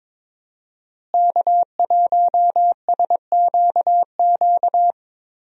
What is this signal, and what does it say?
Morse code, heard as a keyed tone, about 22 wpm, 700 Hz. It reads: K1SQQ
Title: USA Call Signs - 5 Characters - Repeated 3x - 22wpm